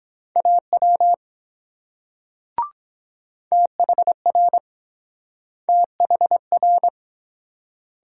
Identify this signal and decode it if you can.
Morse code, heard as a keyed tone, about 24 wpm, 700 Hz. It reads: AW THR THR